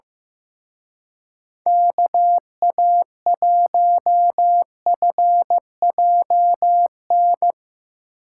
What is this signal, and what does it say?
Morse code, heard as a keyed tone, about 16 wpm, 700 Hz. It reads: KA1FJN
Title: Speed Racing - USA 2x3 Call Signs 15wpm